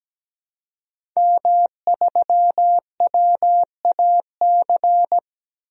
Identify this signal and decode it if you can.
Morse code, heard as a keyed tone, about 17 wpm, 700 Hz. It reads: M3WAC
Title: DXCC-100 Call Signs - 5 Characters - 17wpm